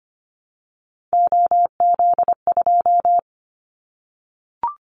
Morse code, heard as a keyed tone, about 25 wpm, 700 Hz. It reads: OZ2
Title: Callsign Prefixes for Top 100 DXCC - 25wpm